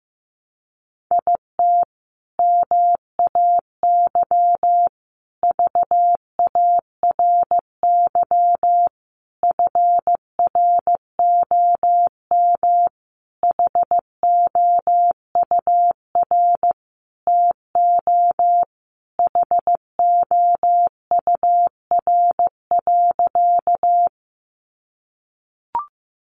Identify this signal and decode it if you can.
Morse code, heard as a keyed tone, about 15 wpm, 700 Hz. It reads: IT MAY VARY FROM HOUR TO HOUR.
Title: Sentences from Top 1200 Words 15wpm